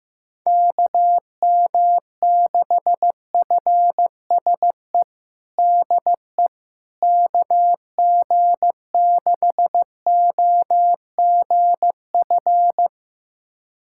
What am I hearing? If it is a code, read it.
Morse code, heard as a keyed tone, about 15 wpm, 700 Hz. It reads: KM6FSE DE KG6OGF
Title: USA Callsign DE Callsign - 6 Characters - 15wpm